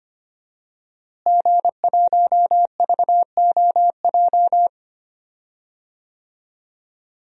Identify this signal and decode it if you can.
Morse code, heard as a keyed tone, about 25 wpm, 700 Hz. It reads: G1VOJ
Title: DXCC-100 Call Signs - 5 Characters - Repeated 3x - 25wpm